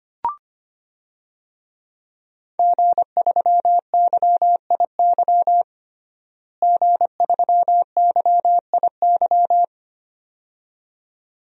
Morse code, heard as a keyed tone, about 25 wpm, 700 Hz. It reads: G3YIY G3YIY
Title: DXCC-100 Call Signs - 5 Characters - Repeated 2x - 25wpm